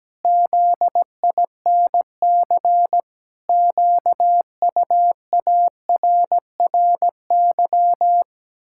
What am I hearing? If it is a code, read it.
Morse code, heard as a keyed tone, about 17 wpm, 700 Hz. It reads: ZINC QUARRY
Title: Pangram Challenge v2 - 17wpm